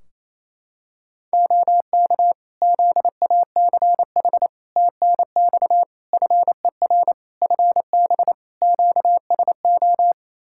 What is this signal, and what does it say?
Morse code, heard as a keyed tone, about 27 wpm, 700 Hz. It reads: OK ZACH TNX FER FB QSO
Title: K1USN SST QSO Practice 28wpm